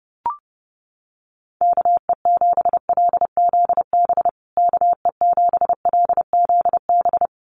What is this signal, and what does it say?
Morse code, heard as a keyed tone, about 30 wpm, 700 Hz. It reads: KE7LZB KE7LZB
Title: USA Call Signs Repeated 2x 30wpm